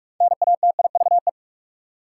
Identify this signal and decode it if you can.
Morse code, heard as a keyed tone, about 46 wpm, 700 Hz. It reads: NATIVE
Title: Six Letter Words - 45wpm